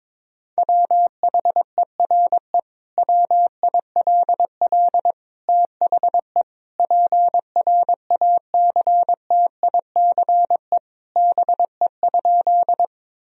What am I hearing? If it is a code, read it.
Morse code, heard as a keyed tone, about 21 wpm, 700 Hz. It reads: WHERE WILL THE PRACTICE BE?
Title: Sentences from Top 1000 Words - Review Part 4 - 22wpm